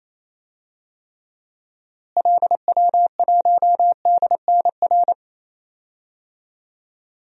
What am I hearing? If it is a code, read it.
Morse code, heard as a keyed tone, about 28 wpm, 700 Hz. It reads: LW1DNR
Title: DXCC-100 Call Signs - 6 Characters - Repeated 2x - 28wpm